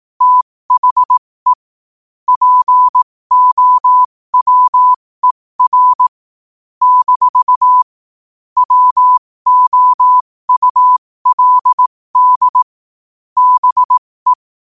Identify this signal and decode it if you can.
Morse code, heard as a keyed tone, about 18 wpm, spaced as for 15 wpm, 1000 Hz. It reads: THE POWER - WOULD BE
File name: qotdfiles / 15wpm